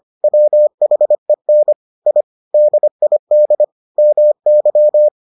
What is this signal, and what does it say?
Morse code, heard as a keyed tone, about 25 wpm, 600 Hz. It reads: WHEN I DID MY